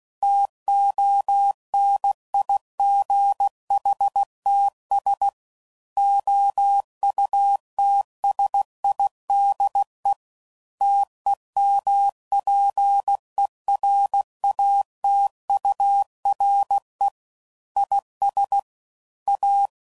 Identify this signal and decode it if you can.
Morse code, heard as a keyed tone, about 16 wpm, 800 Hz. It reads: TONIGHTS OUTSIDE TEMPERATURE IS A
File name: CW Code Groups @15WPM / Weather Sets / Weather 4 10